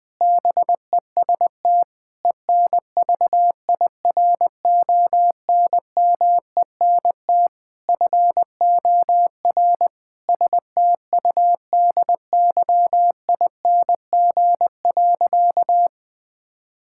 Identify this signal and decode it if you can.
Morse code, heard as a keyed tone, about 20 wpm, 700 Hz. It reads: BEST ENVIRONMENT FOR STUDYING.
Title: Difficult Sentences with Hints - 20wpm